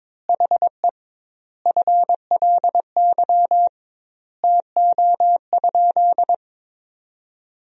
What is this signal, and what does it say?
Morse code, heard as a keyed tone, about 21 wpm, 700 Hz. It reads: HE FLY TO?
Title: 2x Word-Spacing - Sentences from Top 500 Words - 22wpm